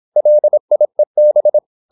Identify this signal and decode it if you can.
Morse code, heard as a keyed tone, about 26 wpm, 600 Hz. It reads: LIEB